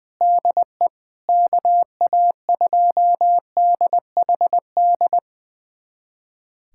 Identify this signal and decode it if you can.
Morse code, heard as a keyed tone, about 19 wpm, 700 Hz. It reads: DE KA2DHD
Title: USA Callsign DE Callsign - 6 Characters - 20wpm